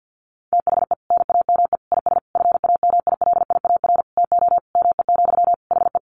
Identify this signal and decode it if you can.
Morse code, heard as a keyed tone, about 48 wpm, 700 Hz. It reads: THE NAME IS FAMILIAR TO ME, HE
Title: Sentences from Top 1000 Words 50wpm